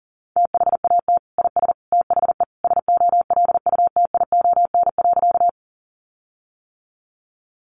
Morse code, heard as a keyed tone, about 40 wpm, 700 Hz. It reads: THAT IS THE SOLUTION.
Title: Sentences from Top 1300 Words 40wpm